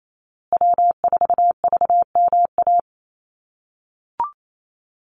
Morse code, heard as a keyed tone, about 28 wpm, 700 Hz. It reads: W4VMA
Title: USA Call Signs Repeated 2x 28wpm